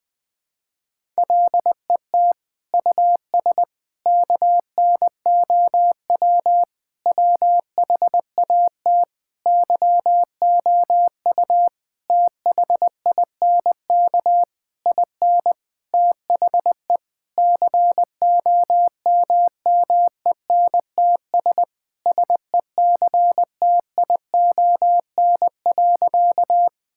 Morse code, heard as a keyed tone, about 20 wpm, 700 Hz. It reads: LET US KNOW WHAT YOU THINK IN THE COMMENTS SECTION.
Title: Sentences with 9-Word Phrases - Spoken Beginning - 20wpm